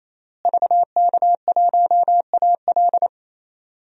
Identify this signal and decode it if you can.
Morse code, heard as a keyed tone, about 28 wpm, 700 Hz. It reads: VK1AL
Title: DXCC-100 Call Signs - 5 Characters - Repeated 3x - 28wpm